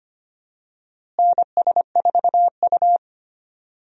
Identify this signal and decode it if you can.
Morse code, heard as a keyed tone, about 25 wpm, 700 Hz. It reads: NS4U
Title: USA Call Signs - 4 Characters - Repeated 3x - 25wpm